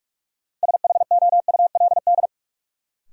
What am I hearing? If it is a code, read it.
Morse code, heard as a keyed tone, about 41 wpm, 700 Hz. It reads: SHOULD
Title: Sets of 2 Words - New Word Drill - 45wpm